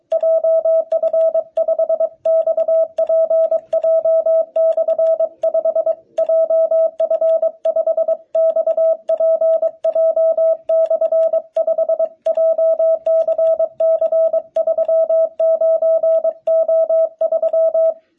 Morse code, heard as a keyed tone, about 23 wpm, 650 Hz. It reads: JF5XPJ/5 JF5XPJ/5 JCC39O3
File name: jh8cbh / 2021cw